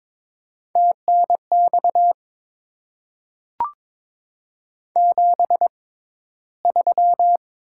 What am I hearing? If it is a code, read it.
Morse code, heard as a keyed tone, about 22 wpm, spaced as for 13 wpm, 700 Hz. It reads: TNX 73
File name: Top 20 QSO Elements - 6x Character Spacing - 22wpm